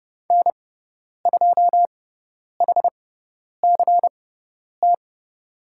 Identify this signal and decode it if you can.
Morse code, heard as a keyed tone, about 30 wpm, spaced as for 10 wpm, 700 Hz. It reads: N2HCT
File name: Lesson 17g - Letter u - Call Signs 30wpm